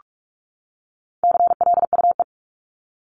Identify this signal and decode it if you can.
Morse code, heard as a keyed tone, about 44 wpm, 700 Hz. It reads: CLUE
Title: Top 1751-2000 Words 45wpm